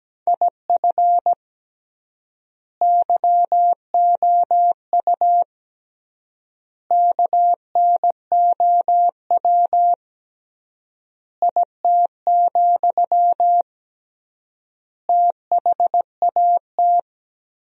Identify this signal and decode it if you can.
Morse code, heard as a keyed tone, about 17 wpm, 700 Hz. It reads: IF YOU KNOW IT, THAT
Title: 3x Word-Spacing - Sentences from Top 400 Words - 17wpm